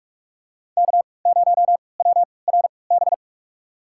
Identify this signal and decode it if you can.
Morse code, heard as a keyed tone, about 41 wpm, spaced as for 25 wpm, 700 Hz. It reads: K0WRB
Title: USA Call Signs - 5 Characters - 3x Character Spacing - 45wpm